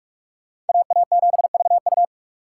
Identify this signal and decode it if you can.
Morse code, heard as a keyed tone, about 59 wpm, spaced as for 52 wpm, 700 Hz. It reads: AA7VU
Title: Speed Racing - USA 2x2 Call Signs 45wpm